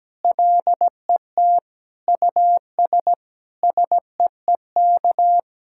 Morse code, heard as a keyed tone, about 17 wpm, 700 Hz. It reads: LET US SEEK
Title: Sentences from Top 1600 Words 17wpm